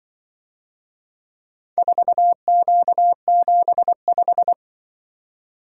Morse code, heard as a keyed tone, about 24 wpm, 700 Hz. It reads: 4Q75
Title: Speed-Racing - Four Letter-Number Combinations - 20wpm